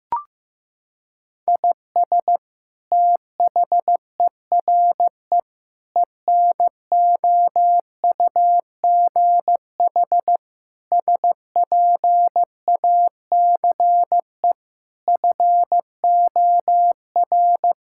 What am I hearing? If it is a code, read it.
Morse code, heard as a keyed tone, about 16 wpm, 700 Hz. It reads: IS THERE ENOUGH SPACE FOR
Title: Sentences from Top 1000 Words - Review Part 5 - 15wpm